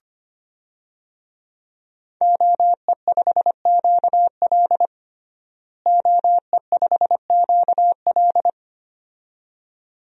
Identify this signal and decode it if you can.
Morse code, heard as a keyed tone, about 25 wpm, 700 Hz. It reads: OE5QL OE5QL
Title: DXCC-100 Call Signs - 5 Characters - Repeated 2x - 25wpm